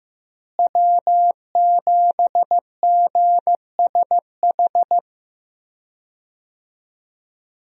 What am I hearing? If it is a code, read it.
Morse code, heard as a keyed tone, about 15 wpm, 700 Hz. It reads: W7GSH
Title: USA Call Signs - 5 Characters - Repeated 2x - 15wpm